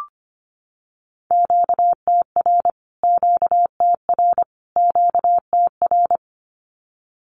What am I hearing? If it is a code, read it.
Morse code, heard as a keyed tone, about 25 wpm, 700 Hz. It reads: QTR QTR QTR
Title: Q-Codes Repeated 3x 25wpm